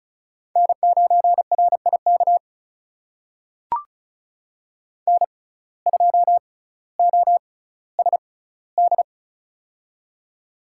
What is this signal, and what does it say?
Morse code, heard as a keyed tone, about 36 wpm, spaced as for 18 wpm, 700 Hz. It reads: N9RIK N2OSD
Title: USA Call Signs - 5 Characters - 6x Character Spacing - 35wpm